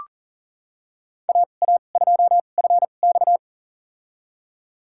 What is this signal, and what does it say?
Morse code, heard as a keyed tone, about 40 wpm, spaced as for 29 wpm, 700 Hz. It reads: AA2FX
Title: USA Call Signs - 5 Characters - 2x Character Spacing - 40wpm